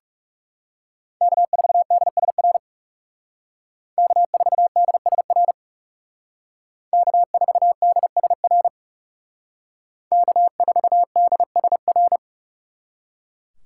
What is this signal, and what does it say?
Morse code, heard as a keyed tone, about 36 wpm, 700 Hz. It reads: K4DSR K4DSR K4DSR K4DSR
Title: Speed Racing - USA 1x3 Call Signs 30wpm